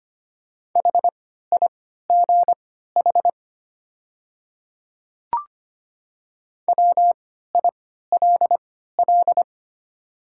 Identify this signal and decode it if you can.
Morse code, heard as a keyed tone, about 25 wpm, spaced as for 14 wpm, 690 Hz. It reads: HIGH WILL
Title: 3x Character Spacing - Top 200 Words - 25wpm